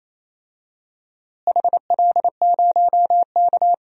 Morse code, heard as a keyed tone, about 28 wpm, 700 Hz. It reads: HL0K
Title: DXCC-100 Call Signs - 4 Characters - Repeated 3x - 28wpm